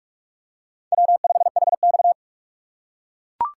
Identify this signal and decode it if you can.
Morse code, heard as a keyed tone, about 43 wpm, 700 Hz. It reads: W5HX
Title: USA Call Signs - 6x Character Spacing - 45wpm